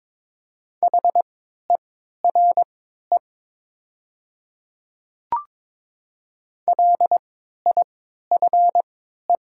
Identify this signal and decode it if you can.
Morse code, heard as a keyed tone, about 22 wpm, spaced as for 12 wpm, 700 Hz. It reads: HERE LIFE
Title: 3x Character Spacing - Top 300 Words - 22wpm